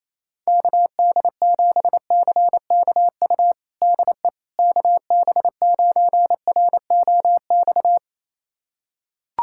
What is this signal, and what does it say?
Morse code, heard as a keyed tone, about 28 wpm, 700 Hz. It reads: KD7CKU DE KB9ROX
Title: USA Callsign DE Callsign 28wpm